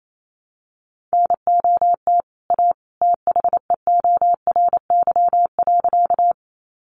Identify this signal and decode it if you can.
Morse code, heard as a keyed tone, about 28 wpm, 700 Hz. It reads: NOT A THEORY.
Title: Sentences from Top 600 Words 28wpm